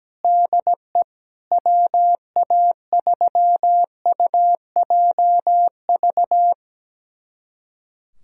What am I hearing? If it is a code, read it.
Morse code, heard as a keyed tone, about 17 wpm, 700 Hz. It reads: DE WA3UJV
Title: USA Callsign DE Callsign 17wpm